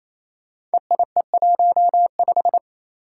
Morse code, heard as a keyed tone, about 27 wpm, 700 Hz. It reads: EIE15
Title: Five Challenging Letter-Number Combinations 28wpm